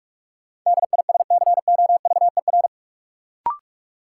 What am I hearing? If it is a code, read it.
Morse code, heard as a keyed tone, about 46 wpm, 700 Hz. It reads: DISCOVER
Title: Sentences from Top 1100 Words - New Word Drill - 45wpm